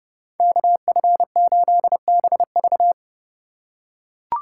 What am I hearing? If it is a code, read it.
Morse code, heard as a keyed tone, about 30 wpm, 700 Hz. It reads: KF8BV